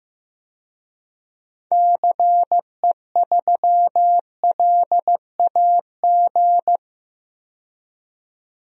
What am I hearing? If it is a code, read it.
Morse code, heard as a keyed tone, about 15 wpm, 700 Hz. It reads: CE3LAG
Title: DXCC-100 Call Signs - 6 Characters - 15wpm